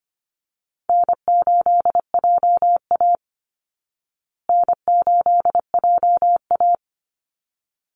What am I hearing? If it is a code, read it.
Morse code, heard as a keyed tone, about 25 wpm, 700 Hz. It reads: N8JA N8JA
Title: Easy Call Signs Repeated 2x - 25wpm